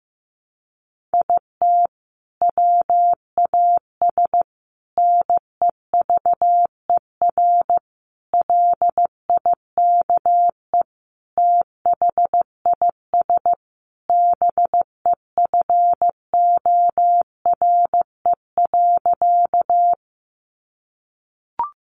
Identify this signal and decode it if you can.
Morse code, heard as a keyed tone, about 15 wpm, 700 Hz. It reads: IT WAS NEVER LIKE THIS BEFORE.
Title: A Sound Of Thunder 15wpm